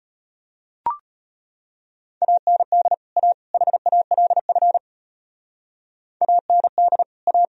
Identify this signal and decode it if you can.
Morse code, heard as a keyed tone, about 31 wpm, 700 Hz. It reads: AND A HALF AND A
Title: Speed-Racing - Sets of 3 Words 25wpm